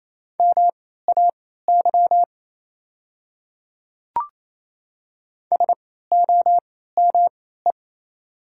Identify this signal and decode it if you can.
Morse code, heard as a keyed tone, about 28 wpm, spaced as for 16 wpm, 690 Hz. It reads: MAY SOME
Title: 3x Character Spacing - Top 100 Words - 28wpm